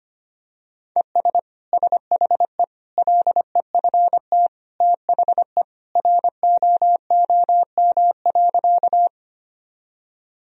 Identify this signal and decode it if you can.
Morse code, heard as a keyed tone, about 25 wpm, 700 Hz. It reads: ES SHE LEFT THE ROOM.
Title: Sentences from Top 300 Words and Abbreviations 25wpm